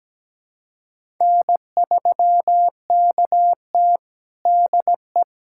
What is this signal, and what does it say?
Morse code, heard as a keyed tone, about 17 wpm, 700 Hz. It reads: N3KT DE
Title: USA Callsign DE Callsign - 4 Characters - 17wpm